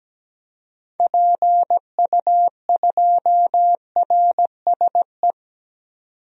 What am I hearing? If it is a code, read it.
Morse code, heard as a keyed tone, about 17 wpm, 700 Hz. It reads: PU2RSE
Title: DXCC-100 Call Signs - 6 Characters - 17wpm